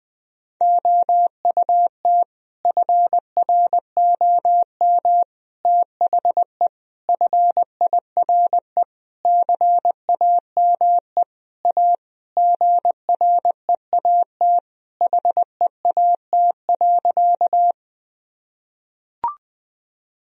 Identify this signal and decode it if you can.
Morse code, heard as a keyed tone, about 20 wpm, 700 Hz. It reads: OUT FROM THE FIRE CAME A GREAT HEAT.
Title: Sentences from Top 1000 Words - Review Part 3 - 20wpm